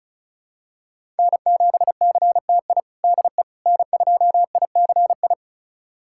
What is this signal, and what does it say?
Morse code, heard as a keyed tone, about 36 wpm, 700 Hz. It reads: N7CTI DE N2ICI
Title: USA Callsign DE Callsign - 5 Characters - 35wpm